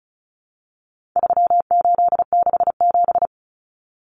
Morse code, heard as a keyed tone, about 34 wpm, 700 Hz. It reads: 3867
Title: Numbers - 4-Digit 35wpm